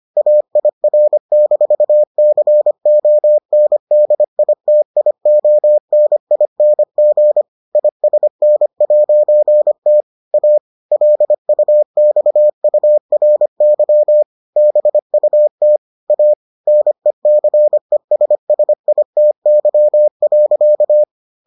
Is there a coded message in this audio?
Yes: Morse code from 0.2 to 21.0 s, heard as a keyed tone, about 25 wpm, 600 Hz. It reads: AIR-CONDITIONING ISN'T A LUXURY BUT A NECESSITY.